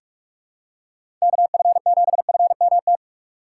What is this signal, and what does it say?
Morse code, heard as a keyed tone, about 46 wpm, 700 Hz. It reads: KF7FMT
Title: USA Call Signs - 6x Character Spacing - 45wpm